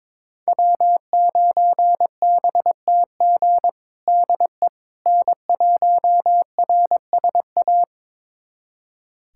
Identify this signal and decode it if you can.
Morse code, heard as a keyed tone, about 22 wpm, 700 Hz. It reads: W9BTG DE N1RSA
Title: USA Callsign DE Callsign - 5 Characters - 22wpm